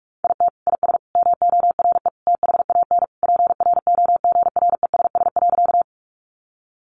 Morse code, heard as a keyed tone, about 44 wpm, 700 Hz. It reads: IT IS MORE THAN PROGRESS.
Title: Sentences from Top 1400 Words 45wpm